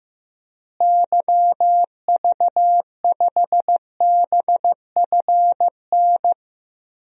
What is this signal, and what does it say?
Morse code, heard as a keyed tone, about 15 wpm, 700 Hz. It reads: YV5BFN
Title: DXCC-100 Call Signs - 6 Characters - Repeated 2x - 15wpm